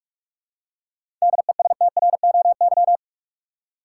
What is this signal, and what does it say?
Morse code, heard as a keyed tone, about 46 wpm, 700 Hz. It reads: DESTROY
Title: Seven Letter Words - 45wpm